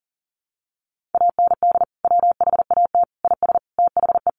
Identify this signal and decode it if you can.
Morse code, heard as a keyed tone, about 40 wpm, 700 Hz. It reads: AND WHAT IS THE
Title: Sentences from Top 1500 Words 40wpm